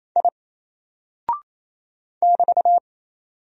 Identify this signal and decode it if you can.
Morse code, heard as a keyed tone, about 27 wpm, 700 Hz. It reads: I <BT>
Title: Single Character (Letter:Number:Punctuation:Prosign) 28wpm